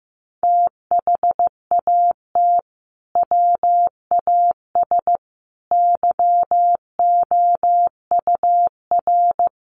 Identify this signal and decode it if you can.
Morse code, heard as a keyed tone, about 15 wpm, 700 Hz. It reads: THAT WAS YOUR
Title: Sentences from Top 600 Words 15wpm